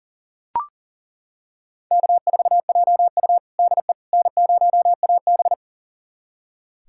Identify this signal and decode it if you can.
Morse code, heard as a keyed tone, about 40 wpm, 700 Hz. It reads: K4JU DE N0AB
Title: USA Callsign DE Callsign - 4 Characters - 40wpm